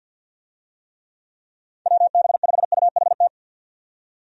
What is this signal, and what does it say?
Morse code, heard as a keyed tone, about 49 wpm, 700 Hz. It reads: WB5UHT
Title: USA Call Signs - 6 Characters - 50wpm